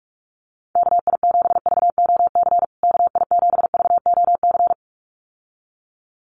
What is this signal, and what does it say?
Morse code, heard as a keyed tone, about 44 wpm, 700 Hz. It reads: KI7VOC KI7VOC
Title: USA Call Signs Repeated 2x 45wpm